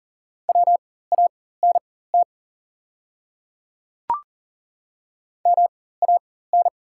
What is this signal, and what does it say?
Morse code, heard as a keyed tone, about 40 wpm, spaced as for 19 wpm, 700 Hz. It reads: WANT MAN
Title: 4x Character Spacing - Top 200 Words - 40wpm